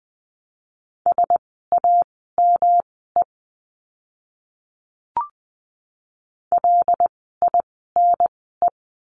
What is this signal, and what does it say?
Morse code, heard as a keyed tone, about 20 wpm, spaced as for 15 wpm, 700 Hz. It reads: SAME LINE
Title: 108 Lesson 18f - Letter m - Words 20wpm